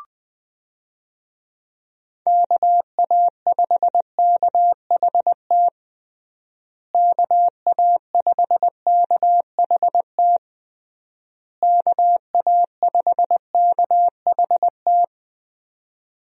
Morse code, heard as a keyed tone, about 20 wpm, 700 Hz. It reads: KA5KHT KA5KHT KA5KHT
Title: USA Call Signs - 6 Characters - Repeated 3x - 20wpm